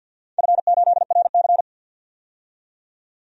Morse code, heard as a keyed tone, about 53 wpm, 700 Hz. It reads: F8RC
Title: DXCC-100 Call Signs - 4 Characters - 50wpm